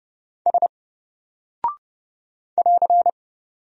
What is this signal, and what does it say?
Morse code, heard as a keyed tone, about 30 wpm, 700 Hz. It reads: S <AR>
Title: Single Character (Letter:Number:Punctuation:Prosign) 30wpm